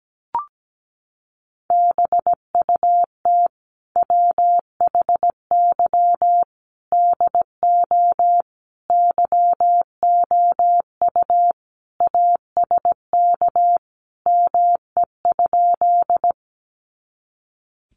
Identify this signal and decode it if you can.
Morse code, heard as a keyed tone, about 17 wpm, 700 Hz. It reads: BUT WHY DO YOU ASK ME?
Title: Sentences from Top 200 Words 17wpm